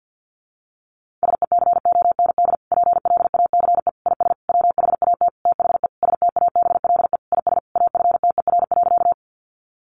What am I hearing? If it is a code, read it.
Morse code, heard as a keyed tone, about 50 wpm, 700 Hz. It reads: SECOND PLACE IS WHAT THE STABLE IS AFTER.
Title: Sentences from Top 1500 Words 50wpm